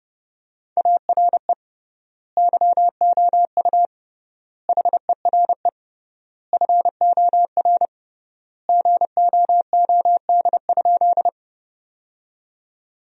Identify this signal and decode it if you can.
Morse code, heard as a keyed tone, about 30 wpm, 700 Hz. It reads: ARE YOU HERE FOR GOOD?
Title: 3x Word-Spacing - Sentences from Top 200 Words - 30wpm